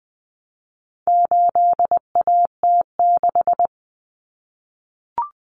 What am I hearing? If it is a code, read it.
Morse code, heard as a keyed tone, about 20 wpm, 700 Hz. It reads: 8AT6
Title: Common Cut Numbers - 4-Digit 20wpm